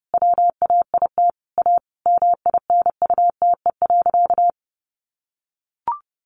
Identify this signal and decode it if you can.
Morse code, heard as a keyed tone, about 30 wpm, 700 Hz. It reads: WAIT A MINUTE.
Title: Sentences from Top 500 Words 30wpm